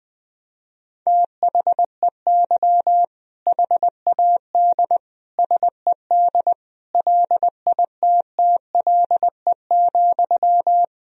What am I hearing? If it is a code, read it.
Morse code, heard as a keyed tone, about 20 wpm, 700 Hz. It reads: THEY HAD SED LITTLE,
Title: Sentences from Top 400 Words and Abbreviations 20wpm